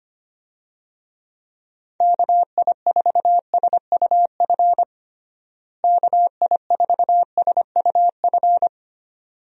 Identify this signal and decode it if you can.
Morse code, heard as a keyed tone, about 25 wpm, 700 Hz. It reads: KI4SUF KI4SUF
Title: USA Call Signs - 6 Characters - Repeated 2x - 25wpm